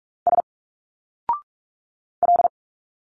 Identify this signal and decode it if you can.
Morse code, heard as a keyed tone, about 46 wpm, 700 Hz. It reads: S L
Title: Single Letters 45wpm